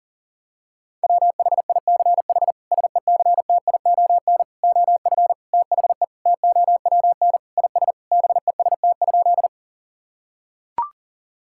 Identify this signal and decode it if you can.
Morse code, heard as a keyed tone, about 40 wpm, 700 Hz. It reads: WHICH SECTION OF THE TOWN IS BEST?
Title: Sentences from Top 1000 Words - Review Part 5 - 40wpm